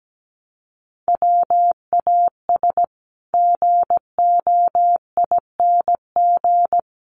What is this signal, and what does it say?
Morse code, heard as a keyed tone, about 17 wpm, 700 Hz. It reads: WAS GOING
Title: Sets of 5 Words Encore2 17wpm